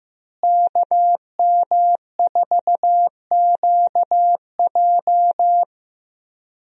Morse code, heard as a keyed tone, about 15 wpm, 700 Hz. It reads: KM4QJ
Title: USA Call Signs - 5 Characters - Repeated 2x - 15wpm